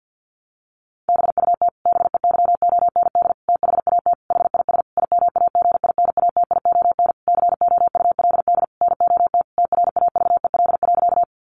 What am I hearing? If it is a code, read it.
Morse code, heard as a keyed tone, about 50 wpm, 700 Hz. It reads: BUT BEYOND THAT HIS IMAGINATION COULD NOT TRAVEL.
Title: Sentences from Top 1300 Words 50wpm